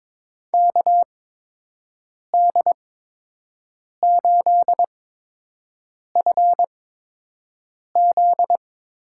Morse code, heard as a keyed tone, about 23 wpm, spaced as for 6 wpm, 700 Hz. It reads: KD8FZ
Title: USA Call Signs - 6 Characters - 8x Character Spacing - 22wpm